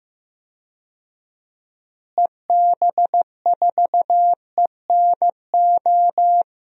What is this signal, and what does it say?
Morse code, heard as a keyed tone, about 15 wpm, 700 Hz. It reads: EB4ENO
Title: DXCC-100 Call Signs - 6 Characters - Repeated 2x - 15wpm